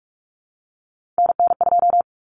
Morse code, heard as a keyed tone, about 44 wpm, 700 Hz. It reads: NN2
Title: Common Cut Numbers - 3-Digit 45wpm